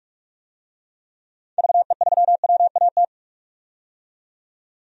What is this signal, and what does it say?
Morse code, heard as a keyed tone, about 41 wpm, 700 Hz. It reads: VE3WAT
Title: DXCC-100 Call Signs - 6 Characters - 45wpm